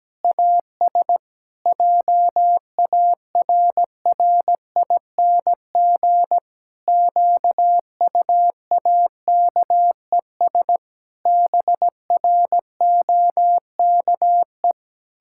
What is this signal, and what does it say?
Morse code, heard as a keyed tone, about 17 wpm, 700 Hz. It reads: AS JARRING QUAKES BROKE
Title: Pangram Challenge v2 - 17wpm